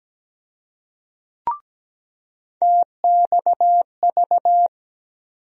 Morse code, heard as a keyed tone, about 17 wpm, 700 Hz. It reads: TXV
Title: USA Call Sign Postfixes - 3 Letters 17wpm